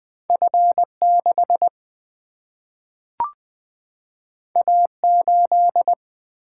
Morse code, heard as a keyed tone, about 20 wpm, 700 Hz. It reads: F6 A8
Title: Single-Letter Single-Number 20wpm